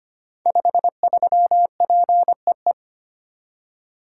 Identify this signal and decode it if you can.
Morse code, heard as a keyed tone, about 25 wpm, 700 Hz. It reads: 53PEE
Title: Code Groups - 5 Letter:Numbers - 25wpm